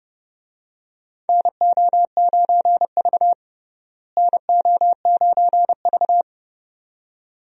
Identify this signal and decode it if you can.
Morse code, heard as a keyed tone, about 29 wpm, 700 Hz. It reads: NO9V NO9V
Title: USA Call Signs - 4 Characters - Repeated 2x - 30wpm